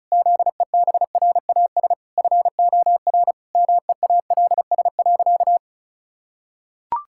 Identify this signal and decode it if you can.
Morse code, heard as a keyed tone, about 36 wpm, 700 Hz. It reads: ZEBRAS FOR MEALS.
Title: Pangram Challenge v2 - 35wpm